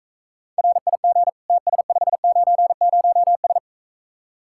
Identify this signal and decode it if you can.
Morse code, heard as a keyed tone, about 45 wpm, 700 Hz. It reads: RIG TS590S